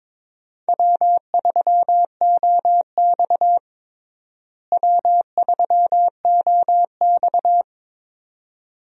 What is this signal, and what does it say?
Morse code, heard as a keyed tone, about 21 wpm, 700 Hz. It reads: W3OX W3OX
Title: USA Call Signs - 4 Characters - Repeated 2x - 22wpm